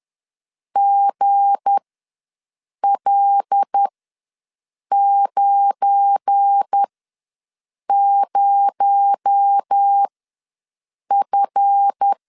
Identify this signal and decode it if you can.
Morse code, heard as a keyed tone, about 10 wpm, spaced as for 6 wpm, 800 Hz. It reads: GL90F
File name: LEZIONI / 7 GF9Z / 7 GF9 ws 10wpm 4034